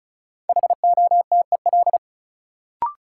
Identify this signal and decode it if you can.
Morse code, heard as a keyed tone, about 36 wpm, 700 Hz. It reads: HOTEL
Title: Sentences from Top 1600 Words - New Word Drill - 35wpm